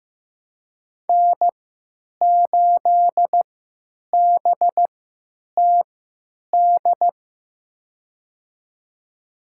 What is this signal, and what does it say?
Morse code, heard as a keyed tone, about 15 wpm, spaced as for 9 wpm, 700 Hz. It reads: N8BTD
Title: USA Call Signs - 3x Character Spacing - 15wpm